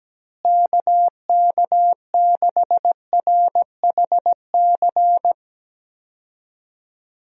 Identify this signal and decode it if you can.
Morse code, heard as a keyed tone, about 17 wpm, 700 Hz. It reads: KK6RHC
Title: USA Call Signs - 6 Characters - Repeated 3x - 17wpm